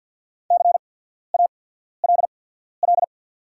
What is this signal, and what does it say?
Morse code, heard as a keyed tone, about 48 wpm, spaced as for 14 wpm, 700 Hz. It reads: CALL
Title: 8x Character Spacing - Top 200 Words - 50wpm